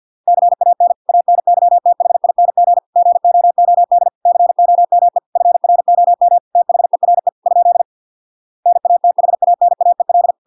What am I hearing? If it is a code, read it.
Morse code, heard as a keyed tone, about 50 wpm, 700 Hz. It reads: CAN ANYTHING GOOD COME FROM THERE ? NATHANAEL